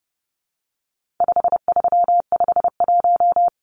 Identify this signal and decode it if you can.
Morse code, heard as a keyed tone, about 30 wpm, 700 Hz. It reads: 5351
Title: Numbers - 4-Digit 30wpm